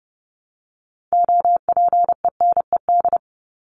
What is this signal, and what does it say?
Morse code, heard as a keyed tone, about 30 wpm, 700 Hz. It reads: OPENED
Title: ED Suffix - Top 100 Words with Base Words 30wpm